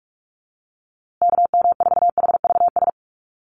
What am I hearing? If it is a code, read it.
Morse code, heard as a keyed tone, about 44 wpm, 690 Hz. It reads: KM4HVS
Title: USA Call Signs Repeated 2x 45wpm